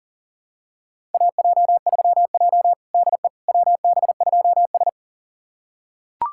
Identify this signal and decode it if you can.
Morse code, heard as a keyed tone, about 40 wpm, 700 Hz. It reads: AJ3J DE WB2S
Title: USA Callsign DE Callsign - 4 Characters - 40wpm